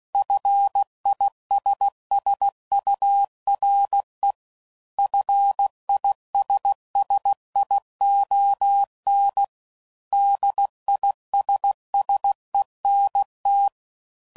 Morse code, heard as a keyed tone, about 16 wpm, 800 Hz. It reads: FISSURE FISSION DISSENT